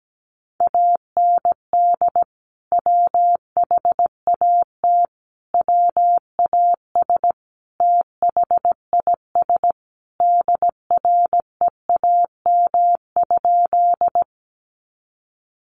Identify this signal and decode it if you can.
Morse code, heard as a keyed tone, about 17 wpm, 700 Hz. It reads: AND WHAT WAS THIS DREAM?
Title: Sentences from Top 1500 Words 17wpm